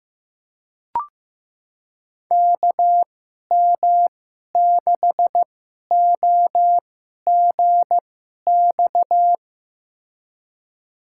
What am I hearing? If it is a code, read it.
Morse code, heard as a keyed tone, about 15 wpm, spaced as for 11 wpm, 700 Hz. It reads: KM6OGX
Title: USA Call Signs - 6 Characters - 2x Character Spacing - 15wpm